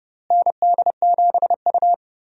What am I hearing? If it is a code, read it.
Morse code, heard as a keyed tone, about 32 wpm, 700 Hz. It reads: ND7U
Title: USA Call Signs - 4 Characters - Repeated 3x - 30wpm